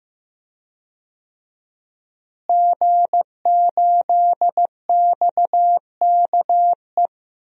Morse code, heard as a keyed tone, about 15 wpm, 700 Hz. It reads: G8XKE
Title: DXCC-100 Call Signs - 5 Characters - Repeated 2x - 15wpm